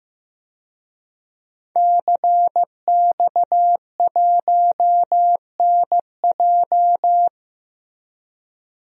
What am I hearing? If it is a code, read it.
Morse code, heard as a keyed tone, about 16 wpm, 700 Hz. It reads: CX1NJ
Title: DXCC-100 Call Signs - 5 Characters - Repeated 2x - 15wpm